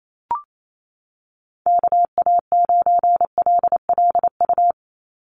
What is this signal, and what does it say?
Morse code, heard as a keyed tone, about 28 wpm, 700 Hz. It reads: KA9LLU
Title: USA Call Signs 28wpm